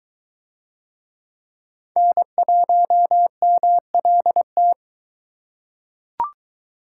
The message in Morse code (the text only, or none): N1MLT